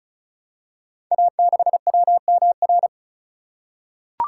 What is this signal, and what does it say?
Morse code, heard as a keyed tone, about 36 wpm, 700 Hz. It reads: A6WMR
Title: Code Groups - 5 Letter:Numbers - 35wpm